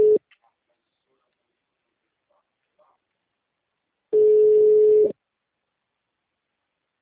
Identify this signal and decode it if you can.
Ringback tone (Europe, 425 Hz).
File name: out- -2028- - - .26952